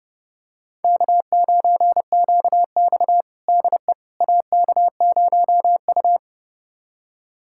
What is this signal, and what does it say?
Morse code, heard as a keyed tone, about 30 wpm, 700 Hz. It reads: K9QX DE AK0U
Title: USA Callsign DE Callsign - 4 Characters - 30wpm